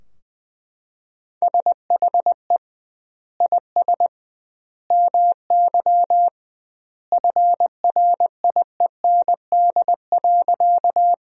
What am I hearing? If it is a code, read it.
Morse code, heard as a keyed tone, about 21 wpm, 700 Hz. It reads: SHE IS MY FRIEND.
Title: 2x Word-Spacing - Sentences from Top 300 Words - 20wpm